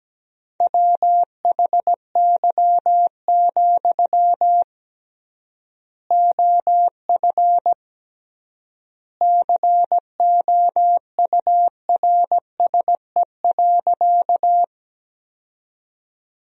Morse code, heard as a keyed tone, about 17 wpm, 700 Hz. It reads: WHY, OF COURSE.
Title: 3x Word-Spacing - Sentences from Top 500 Words - 17wpm